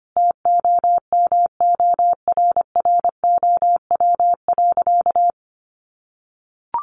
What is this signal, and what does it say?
Morse code, heard as a keyed tone, about 25 wpm, 700 Hz. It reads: TOMORROW.
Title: Additional Easy Phrases 25wpm